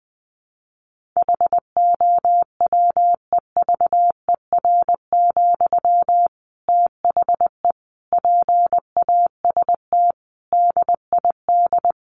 Morse code, heard as a keyed tone, about 20 wpm, 700 Hz. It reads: HOWEVER, THE PAST DID
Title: Sentences from Top 1400 Words 20wpm